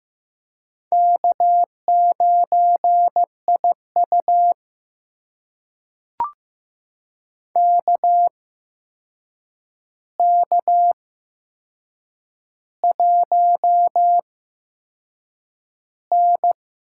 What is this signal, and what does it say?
Morse code, heard as a keyed tone, about 15 wpm, spaced as for 6 wpm, 700 Hz. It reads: K9IU KK1N
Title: USA Call Signs - 4 Characters - 8x Character Spacing - 15wpm